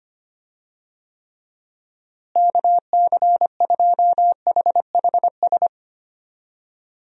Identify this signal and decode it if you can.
Morse code, heard as a keyed tone, about 25 wpm, 700 Hz. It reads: KC2HHS
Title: USA Call Signs - 6 Characters - Repeated 2x - 25wpm